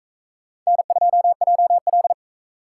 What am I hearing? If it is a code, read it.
Morse code, heard as a keyed tone, about 41 wpm, 700 Hz. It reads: N2JL